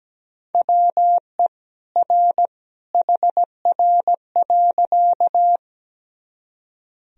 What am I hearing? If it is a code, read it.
Morse code, heard as a keyed tone, about 17 wpm, 700 Hz. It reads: WE R HR.
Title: Sentences from Top 200 Words and Abbreviations 17wpm